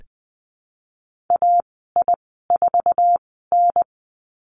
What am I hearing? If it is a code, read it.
Morse code, heard as a keyed tone, about 20 wpm, spaced as for 15 wpm, 700 Hz. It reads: AI4N
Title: 069 Lesson 12g - Letter d - Call Signs 20wpm